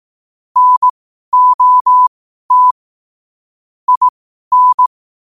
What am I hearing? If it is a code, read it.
Morse code, heard as a keyed tone, about 18 wpm, spaced as for 12 wpm, 1000 Hz. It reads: NOT IN
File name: qotdfiles / 12wpm